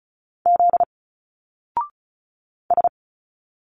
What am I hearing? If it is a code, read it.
Morse code, heard as a keyed tone, about 33 wpm, 700 Hz. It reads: Z S
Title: Single Letters 35wpm